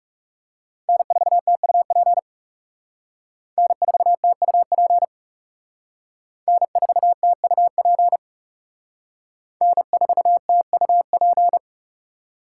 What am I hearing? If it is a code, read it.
Morse code, heard as a keyed tone, about 36 wpm, 700 Hz. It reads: N4TUP N4TUP N4TUP N4TUP
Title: Speed Racing - USA 1x3 Call Signs 30wpm